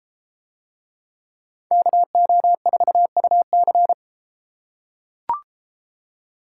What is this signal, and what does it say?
Morse code, heard as a keyed tone, about 33 wpm, 700 Hz. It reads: KO4UC